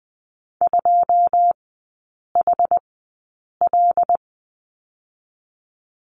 Morse code, heard as a keyed tone, about 20 wpm, spaced as for 8 wpm, 700 Hz. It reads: 2HL
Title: 121 Lesson 20e - Number 3 - Three Characters 20wpm